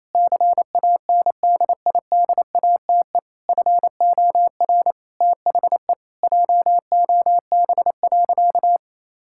Morse code, heard as a keyed tone, about 28 wpm, 700 Hz. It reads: CANDIDATE FOR THE JOB.
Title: Sentences from Top 1600 Words 28wpm